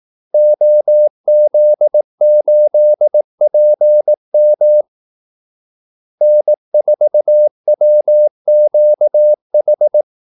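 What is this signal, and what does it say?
Morse code, heard as a keyed tone, about 18 wpm, 600 Hz. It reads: OZ8PM N4WQH
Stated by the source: Morse call-sign practice at 18 wpm.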